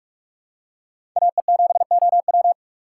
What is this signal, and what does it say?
Morse code, heard as a keyed tone, about 43 wpm, 700 Hz. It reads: AE7OW
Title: USA Call Signs - 5 Characters - 4x Character Spacing - 45wpm